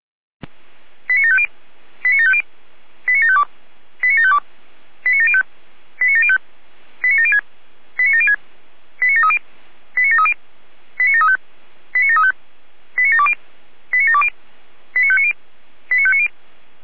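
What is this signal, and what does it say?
Five-tone selective call (ZVEI), heard as quick runs of five tones, tones of about 70 ms. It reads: 89750 89750 89752 89752 89705 89705 89706 89706 89730 89730 89735 89735 89720 89720 89590 89590
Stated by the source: ZVEI five-tone selective call received on SDR.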